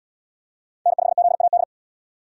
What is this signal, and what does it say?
Morse code, heard as a keyed tone, about 74 wpm, 700 Hz. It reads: N5ZAD